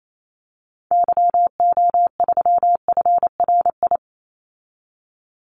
Morse code, heard as a keyed tone, about 28 wpm, 700 Hz. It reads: YO3FRI
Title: Call Signs Repeated 3x - 28wpm